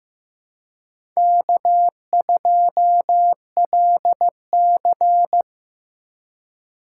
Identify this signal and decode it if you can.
Morse code, heard as a keyed tone, about 15 wpm, 700 Hz. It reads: K2LC
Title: USA Call Signs - 4 Characters - 3x Character Spacing - 15wpm